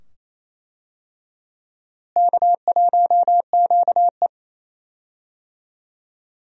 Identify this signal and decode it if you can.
Morse code, heard as a keyed tone, about 28 wpm, 700 Hz. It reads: K1QE
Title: USA Call Signs - 4 Characters - Repeated 2x - 28wpm